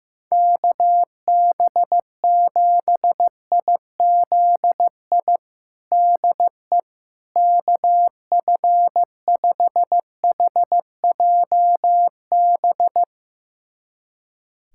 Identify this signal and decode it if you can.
Morse code, heard as a keyed tone, about 15 wpm, 700 Hz. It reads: KB7IZI DE KF5HJB
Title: USA Callsign DE Callsign - 6 Characters - 15wpm